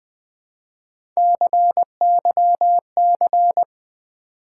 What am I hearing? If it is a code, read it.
Morse code, heard as a keyed tone, about 21 wpm, 700 Hz. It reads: CYC
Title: Build A Word - 1 Letter At A Time - 3 to 5-Letter Words - 20wpm